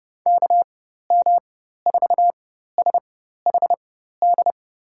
Morse code, heard as a keyed tone, about 30 wpm, spaced as for 14 wpm, 700 Hz. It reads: KM4SHD
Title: USA Call Signs - 6 Characters - 4x Character Spacing - 30wpm